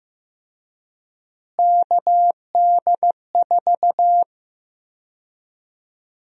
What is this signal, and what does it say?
Morse code, heard as a keyed tone, about 15 wpm, 700 Hz. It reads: KD4
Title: USA Call Sign Prefixes - Repeated 2x - 15wpm